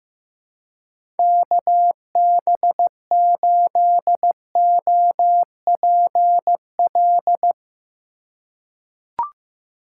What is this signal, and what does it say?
Morse code, heard as a keyed tone, about 15 wpm, 700 Hz. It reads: KB8OPL
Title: USA Call Signs - 6 Characters - 3x Character Spacing - 15wpm